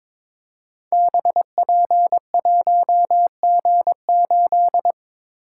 Morse code, heard as a keyed tone, about 23 wpm, 700 Hz. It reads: BP1G8
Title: Code Groups - 5 Letter:Numbers - 22wpm